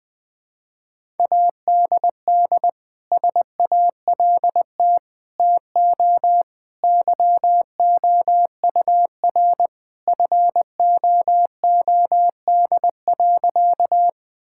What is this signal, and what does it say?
Morse code, heard as a keyed tone, about 20 wpm, 700 Hz. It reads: ADD SALT TO YOUR FOOD.
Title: Sentences from Top 1000 Words - Review Part 5 - 20wpm